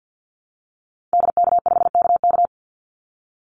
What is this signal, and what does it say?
Morse code, heard as a keyed tone, about 50 wpm, 700 Hz. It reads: DK5KK
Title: Call Signs Repeated 2x - 50wpm